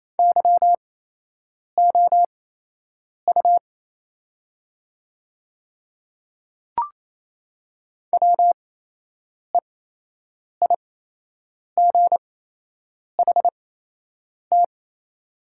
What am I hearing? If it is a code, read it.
Morse code, heard as a keyed tone, about 28 wpm, spaced as for 8 wpm, 700 Hz. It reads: YOU WEIGHT
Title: 8x Character Spacing - Top 500 Words - 28wpm